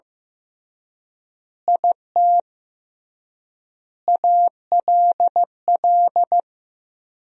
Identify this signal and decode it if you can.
Morse code, heard as a keyed tone, about 15 wpm, 700 Hz. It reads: IT ALL
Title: 3x Word-Spacing - Sentences from Top 200 Words - 15wpm